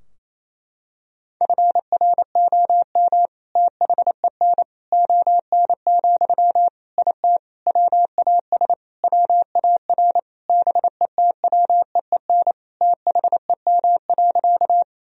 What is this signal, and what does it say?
Morse code, heard as a keyed tone, about 28 wpm, 700 Hz. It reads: FROM THEN ON, IT WAS WAR BETWEEN THEM.
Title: Sentences from Top 1000 Words - Review Part 2 - 28wpm